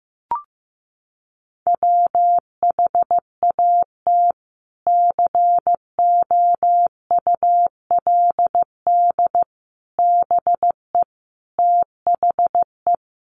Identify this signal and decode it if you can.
Morse code, heard as a keyed tone, about 15 wpm, 700 Hz. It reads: WHAT COULD BE THE
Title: Sentences from Top 900 Words 15wpm